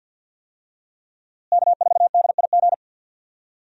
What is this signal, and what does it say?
Morse code, heard as a keyed tone, about 52 wpm, 700 Hz. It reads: K4DIG
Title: USA Call Signs - 5 Characters - Repeated 2x - 50wpm